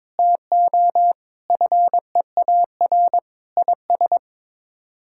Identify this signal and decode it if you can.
Morse code, heard as a keyed tone, about 22 wpm, 700 Hz. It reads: TO FEAR IS
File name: practice / 22wpm / Sets of 8 Words - 22wpm